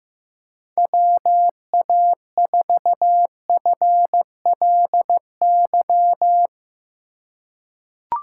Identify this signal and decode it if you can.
Morse code, heard as a keyed tone, about 15 wpm, 700 Hz. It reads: WA4FLY
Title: USA Call Signs - 6 Characters - 2x Character Spacing - 15wpm